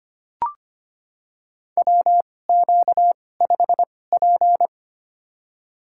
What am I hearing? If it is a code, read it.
Morse code, heard as a keyed tone, about 25 wpm, spaced as for 18 wpm, 700 Hz. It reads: WQ5P
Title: USA Call Signs - 4 Characters - 2x Character Spacing - 25wpm